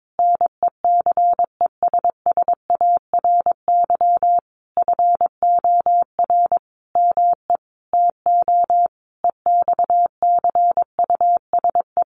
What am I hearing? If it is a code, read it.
Morse code, heard as a keyed tone, about 22 wpm, 700 Hz. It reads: NECESSARY FOR ME TO EXCUSE
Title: Sentences from Top 1300 Words 22wpm